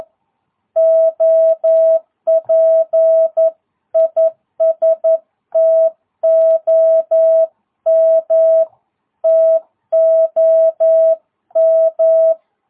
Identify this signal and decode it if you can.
Morse code, heard as a keyed tone, about 11 wpm, 650 Hz. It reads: OP IS TOM TOM
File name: jh8cbh / 2022cw